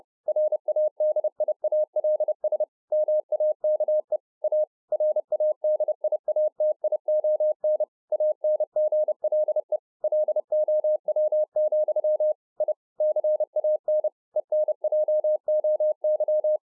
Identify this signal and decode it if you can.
Morse code, heard as a keyed tone, about 30 wpm, 610 Hz. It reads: RADIALS MAKE A RADIATION ANGLE LOW, I CAN ENJOY